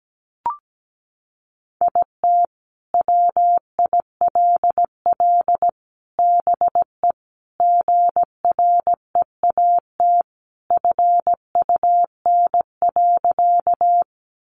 Morse code, heard as a keyed tone, about 17 wpm, 700 Hz. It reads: IT WILL BE GREAT FUN.
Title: Sentences from Top 700 Words 17wpm